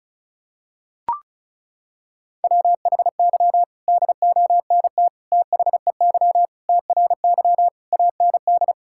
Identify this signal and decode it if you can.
Morse code, heard as a keyed tone, about 36 wpm, 700 Hz. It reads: WHY DONT THEY TRY AND
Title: Sentences from Top 1000 Words - Review Part 5 - 35wpm